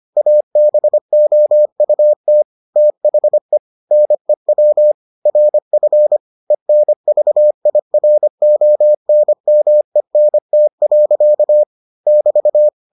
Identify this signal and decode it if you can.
Morse code, heard as a keyed tone, about 25 wpm, 600 Hz. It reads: ABOUT THE NEW RF ENVIRONMENT. <BT>